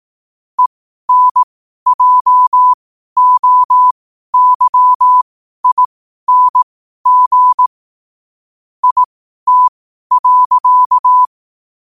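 Morse code, heard as a keyed tone, about 18 wpm, spaced as for 12 wpm, 1000 Hz. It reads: ENJOYING IT.
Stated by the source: Morse code sent at 12 words per minute.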